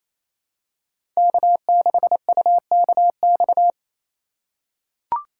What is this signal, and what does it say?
Morse code, heard as a keyed tone, about 28 wpm, 690 Hz. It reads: K6UKX
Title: USA Call Signs - 2x Character Spacing - 28wpm